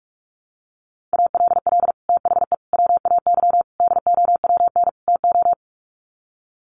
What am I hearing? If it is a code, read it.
Morse code, heard as a keyed tone, about 44 wpm, 700 Hz. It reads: ALL THE WAY DOWN TO
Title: Sets of 5 Words Encore2 45wpm